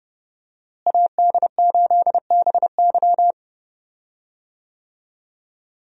Morse code, heard as a keyed tone, about 30 wpm, 700 Hz. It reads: AD8BY
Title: USA Call Signs - 5 Characters - Repeated 2x - 30wpm